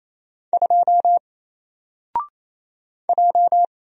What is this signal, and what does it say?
Morse code, heard as a keyed tone, about 27 wpm, 700 Hz. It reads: 2 J
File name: Single-Letter-Number-28wpm